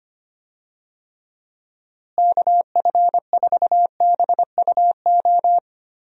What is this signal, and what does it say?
Morse code, heard as a keyed tone, about 25 wpm, 700 Hz. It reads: KF4BUO
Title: USA Call Signs - 6 Characters - Repeated 3x - 25wpm